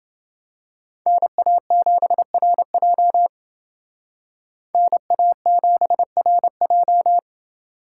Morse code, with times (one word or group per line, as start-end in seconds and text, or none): 1.1-3.3 NA7RJ
4.7-7.2 NA7RJ